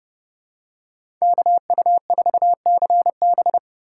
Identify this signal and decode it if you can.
Morse code, heard as a keyed tone, about 30 wpm, 700 Hz. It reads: KU4CB
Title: USA Call Signs - 5 Characters - 4x Character Spacing - 30wpm